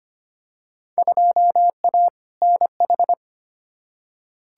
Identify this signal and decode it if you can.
Morse code, heard as a keyed tone, about 25 wpm, 700 Hz. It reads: 2A NH
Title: Field Day Exchanges - 25wpm